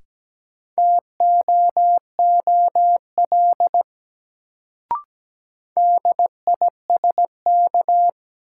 Morse code, heard as a keyed tone, about 17 wpm, 700 Hz. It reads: TOOL DISK
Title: Sentences from Top 1000 Words - Review Part 6 - New Word Drill - 17wpm